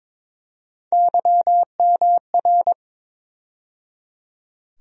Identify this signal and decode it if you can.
Morse code, heard as a keyed tone, about 22 wpm, 700 Hz. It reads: YMR
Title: USA Call Sign Postfixes - 3 Letters 22wpm